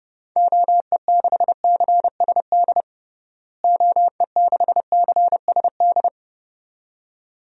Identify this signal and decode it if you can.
Morse code, heard as a keyed tone, about 32 wpm, 700 Hz. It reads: OE6CSD OE6CSD
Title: DXCC-100 Call Signs - 6 Characters - Repeated 2x - 30wpm